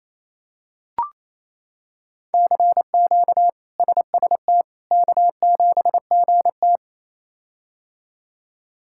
Morse code, heard as a keyed tone, about 28 wpm, 700 Hz. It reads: CQ SST K7GT
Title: K1USN SST QSO Practice - 28wpm